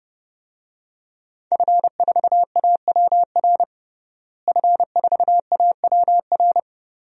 Morse code, heard as a keyed tone, about 32 wpm, 700 Hz. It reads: F4AWR F4AWR
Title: DXCC-100 Call Signs - 5 Characters - Repeated 2x - 30wpm